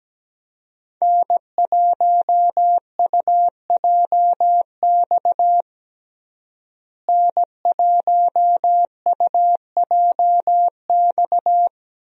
Morse code, heard as a keyed tone, about 17 wpm, 700 Hz. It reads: N1UJX N1UJX
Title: USA Call Signs - 5 Characters - Repeated 2x - 17wpm